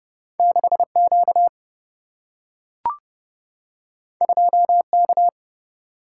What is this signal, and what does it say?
Morse code, heard as a keyed tone, about 30 wpm, 700 Hz. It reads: 6Q 2K
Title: Single-Letter Single-Number 30wpm